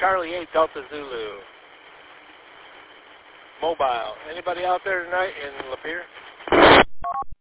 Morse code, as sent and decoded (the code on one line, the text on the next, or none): none